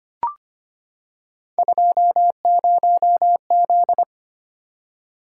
Morse code, heard as a keyed tone, about 25 wpm, 700 Hz. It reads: 20Z
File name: Three Letter-Number Combinations - 25wpm